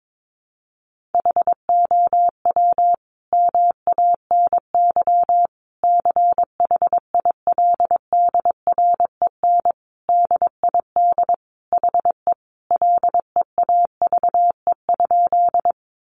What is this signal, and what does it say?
Morse code, heard as a keyed tone, about 22 wpm, 700 Hz. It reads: HOW MANY CHILDREN DID HE LEAVE?
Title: Sentences from Top 400 Words 22wpm